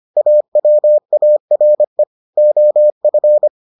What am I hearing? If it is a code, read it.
Morse code, heard as a keyed tone, about 25 wpm, 600 Hz. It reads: AWARE OF